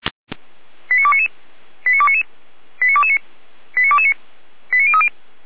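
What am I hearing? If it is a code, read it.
Five-tone selective call (ZVEI), heard as quick runs of five tones, tones of about 70 ms. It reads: 89299 89299 89229 89229 89030
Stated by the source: ZVEI five-tone selective call received with an SDR.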